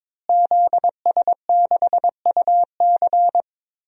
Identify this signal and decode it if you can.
Morse code, heard as a keyed tone, about 22 wpm, 700 Hz. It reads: ZS6UC
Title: DXCC-100 Call Signs - 5 Characters - Repeated 2x - 22wpm